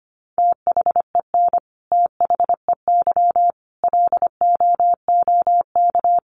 Morse code, heard as a keyed tone, about 25 wpm, 700 Hz. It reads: THEN THEY LOOK
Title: Sentences from Top 1300 Words 25wpm